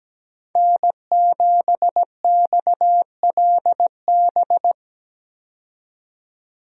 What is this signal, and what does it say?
Morse code, heard as a keyed tone, about 17 wpm, 700 Hz. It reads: N7XLB
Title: USA Call Signs - 5 Characters - 17wpm